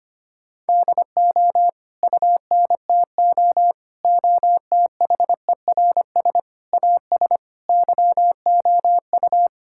Morse code, heard as a keyed tone, about 25 wpm, 700 Hz. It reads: DO UNTO OTHERS AS YOU
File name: Sets of 9 Words - 25wpm